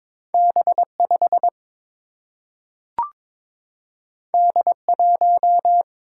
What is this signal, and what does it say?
Morse code, heard as a keyed tone, about 22 wpm, 700 Hz. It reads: B 5 D 1
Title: Single-Letter Single-Number 22wpm